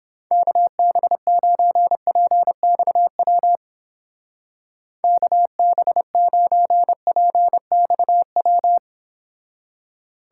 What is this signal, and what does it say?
Morse code, heard as a keyed tone, about 29 wpm, 700 Hz. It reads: KB9PXW KB9PXW